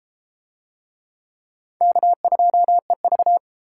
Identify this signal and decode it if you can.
Morse code, heard as a keyed tone, about 31 wpm, 700 Hz. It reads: K2EV